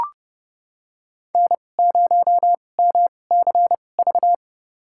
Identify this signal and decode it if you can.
Morse code, heard as a keyed tone, about 31 wpm, spaced as for 22 wpm, 700 Hz. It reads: N0MCV
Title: USA Call Signs - 2x Character Spacing - 30wpm